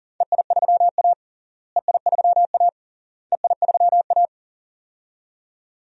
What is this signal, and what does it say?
Morse code, heard as a keyed tone, about 40 wpm, 700 Hz. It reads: EI3A EI3A EI3A
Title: DXCC-100 Call Signs - 4 Characters - Repeated 3x - 40wpm